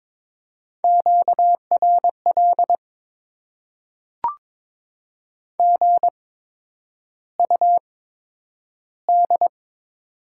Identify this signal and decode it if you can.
Morse code, heard as a keyed tone, about 23 wpm, spaced as for 9 wpm, 700 Hz. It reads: QRL GUD
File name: Top 20 QSO Elements - 8x Character Spacing - 22wpm